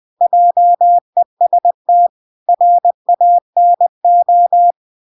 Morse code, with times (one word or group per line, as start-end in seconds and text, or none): 0.2-2.1 JEST
2.5-4.7 RANO